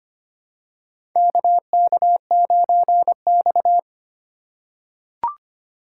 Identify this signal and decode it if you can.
Morse code, heard as a keyed tone, about 25 wpm, 700 Hz. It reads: KK9X
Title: USA Call Signs - 2x Character Spacing - 25wpm